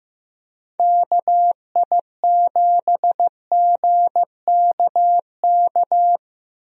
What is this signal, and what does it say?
Morse code, heard as a keyed tone, about 15 wpm, 700 Hz. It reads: KI7GKK
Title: USA Call Signs - 6 Characters - Repeated 3x - 15wpm